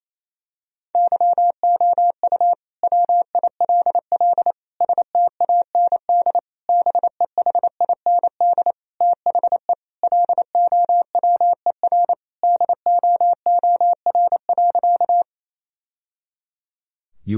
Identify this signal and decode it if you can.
Morse code, heard as a keyed tone, about 28 wpm, 700 Hz. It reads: YOU WILL STAND BEHIND THE LOWER DOOR.
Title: Sentences from Top 1000 Words - Review Part 4 - 28wpm